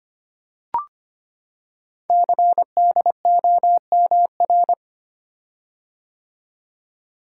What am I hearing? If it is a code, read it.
Morse code, heard as a keyed tone, about 25 wpm, 700 Hz. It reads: CDOMR
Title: Code Groups - 5 Letters - 25wpm